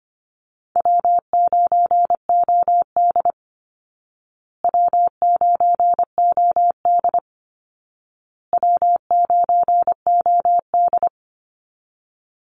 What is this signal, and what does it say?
Morse code, heard as a keyed tone, about 25 wpm, 700 Hz. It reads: W9OD W9OD W9OD
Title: Easy Call Signs Repeated 3x - 25wpm